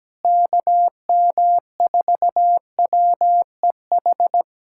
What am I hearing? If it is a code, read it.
Morse code, heard as a keyed tone, about 17 wpm, 700 Hz. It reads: KM4WEH
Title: USA Callsign DE Callsign 17wpm